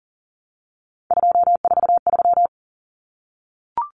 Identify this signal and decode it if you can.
Morse code, heard as a keyed tone, about 40 wpm, 700 Hz. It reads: 243
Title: Numbers - 3-Digit 40wpm